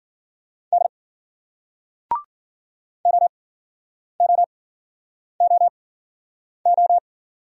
Morse code, heard as a keyed tone, about 56 wpm, spaced as for 9 wpm, 700 Hz. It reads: D OOOO